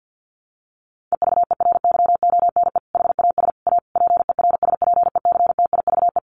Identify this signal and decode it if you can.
Morse code, heard as a keyed tone, about 50 wpm, 700 Hz. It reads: EVERYONE HAS A PERSPECTIVE
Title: Sentences from Top 1200 Words 50wpm